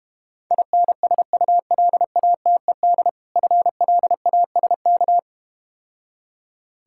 Morse code, heard as a keyed tone, about 32 wpm, 700 Hz. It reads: INSULATED FLASK